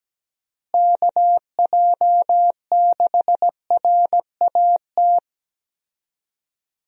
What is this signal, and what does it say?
Morse code, heard as a keyed tone, about 17 wpm, 700 Hz. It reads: KJ6RAT
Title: USA Call Signs - 6 Characters - Repeated 3x - 17wpm